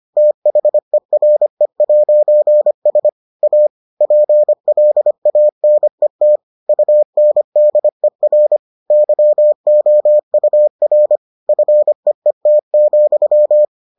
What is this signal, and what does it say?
Morse code, heard as a keyed tone, about 25 wpm, 600 Hz. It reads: THERE'S A PLANET UNDER YOUR FEET,